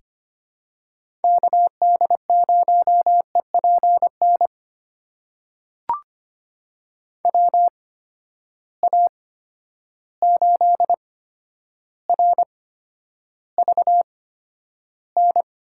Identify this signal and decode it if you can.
Morse code, heard as a keyed tone, about 25 wpm, spaced as for 11 wpm, 700 Hz. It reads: KD0EPN WA8RVN
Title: USA Call Signs - 6 Characters - 8x Character Spacing - 25wpm